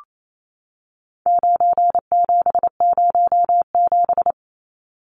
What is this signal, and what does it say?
Morse code, heard as a keyed tone, about 28 wpm, 700 Hz. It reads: 9707
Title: Numbers - 4-Digit 28wpm